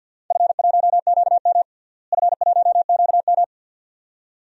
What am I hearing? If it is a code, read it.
Morse code, heard as a keyed tone, about 52 wpm, 700 Hz. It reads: F1QM F1QM
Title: DXCC-100 Call Signs - 4 Characters - Repeated 2x - 50wpm